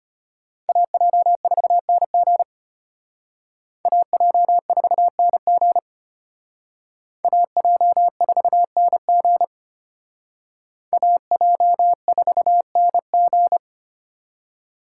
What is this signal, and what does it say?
Morse code, heard as a keyed tone, about 31 wpm, 700 Hz. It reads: AJ4NG AJ4NG AJ4NG AJ4NG
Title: Speed Racing - USA 2x2 Call Signs 25wpm